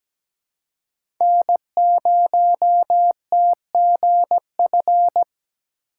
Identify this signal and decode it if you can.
Morse code, heard as a keyed tone, about 17 wpm, 700 Hz. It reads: N0TGF
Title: USA Call Signs - 5 Characters - 3x Character Spacing - 17wpm